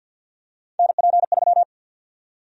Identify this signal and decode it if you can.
Morse code, heard as a keyed tone, about 52 wpm, 700 Hz. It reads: NP3
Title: USA Call Sign Prefixes - 50wpm